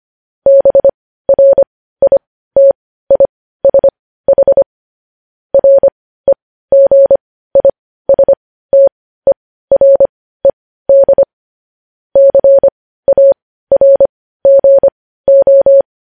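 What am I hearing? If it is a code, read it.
Morse code, heard as a keyed tone, about 25 wpm, spaced as for 15 wpm, 550 Hz. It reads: BRITISH REGISTERED CARGO
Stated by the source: Morse code news headlines sent at 15 wpm.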